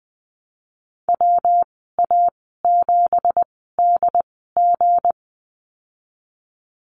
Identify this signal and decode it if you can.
Morse code, heard as a keyed tone, about 20 wpm, spaced as for 15 wpm, 700 Hz. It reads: WA7DG
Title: 173 Lesson 28g - Number 9 - Call Signs 20wpm